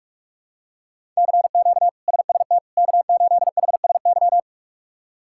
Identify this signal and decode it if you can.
Morse code, heard as a keyed tone, about 46 wpm, 700 Hz. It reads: CQ SST K8HSY
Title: K1USN SST QSO Practice - 45wpm